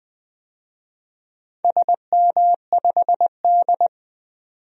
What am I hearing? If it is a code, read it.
Morse code, heard as a keyed tone, about 20 wpm, 700 Hz. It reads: SM5D
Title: DXCC-100 Call Signs - 4 Characters - 20wpm